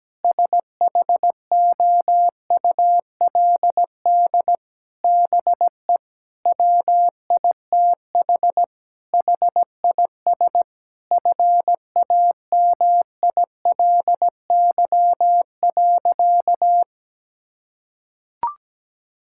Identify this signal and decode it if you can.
Morse code, heard as a keyed tone, about 17 wpm, 700 Hz. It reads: SHOULD BE WITH HIS FAMILY.
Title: Sentences from Top 1600 Words 17wpm